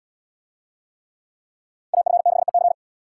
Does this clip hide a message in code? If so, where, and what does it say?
Yes: Morse code from 1.9 to 2.7 s, heard as a keyed tone, about 77 wpm, 700 Hz. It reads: AF6EC